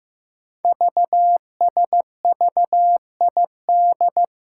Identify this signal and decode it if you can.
Morse code, heard as a keyed tone, about 15 wpm, 700 Hz. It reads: VSVID
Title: Code Groups - 5 Letters - 15wpm